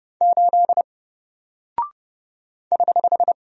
Morse code, heard as a keyed tone, about 30 wpm, 700 Hz. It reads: 8 <HH>
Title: Single Character (Letter:Number:Punctuation:Prosign) Alt 30wpm